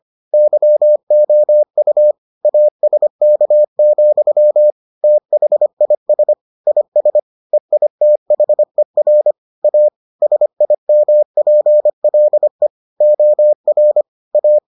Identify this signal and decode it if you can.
Morse code, heard as a keyed tone, about 25 wpm, 600 Hz. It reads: YOU ASK, THIS IS EITHER A SIMPLE OR A